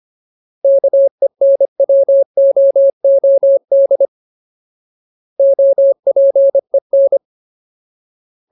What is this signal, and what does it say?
Morse code, heard as a keyed tone, about 25 wpm, 550 Hz. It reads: KENWOOD OPEN
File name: Ultimate Word List / 25WPM Ultimate